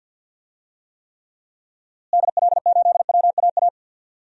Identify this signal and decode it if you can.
Morse code, heard as a keyed tone, about 46 wpm, 700 Hz. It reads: DL8WAA
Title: DXCC-100 Call Signs - 6 Characters - Repeated 2x - 50wpm